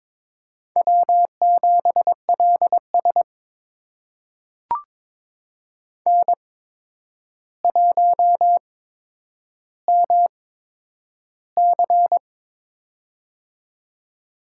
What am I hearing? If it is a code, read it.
Morse code, heard as a keyed tone, about 22 wpm, spaced as for 9 wpm, 700 Hz. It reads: W7LS N1MC
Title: USA Call Signs - 4 Characters - 8x Character Spacing - 22wpm